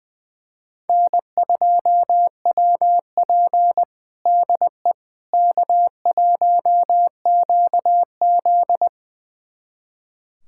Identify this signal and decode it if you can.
Morse code, heard as a keyed tone, about 20 wpm, 700 Hz. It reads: N2WP DE K1QZ
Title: USA Callsign DE Callsign - 4 Characters - 20wpm